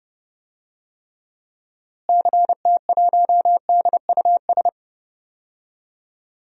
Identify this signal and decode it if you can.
Morse code, heard as a keyed tone, about 30 wpm, 700 Hz. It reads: CT1DUS
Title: DXCC-100 Call Signs - 6 Characters - Repeated 2x - 30wpm